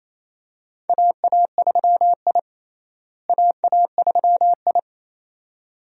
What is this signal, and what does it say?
Morse code, heard as a keyed tone, about 28 wpm, 700 Hz. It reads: AA3I AA3I
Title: USA Call Signs - 4 Characters - Repeated 2x - 28wpm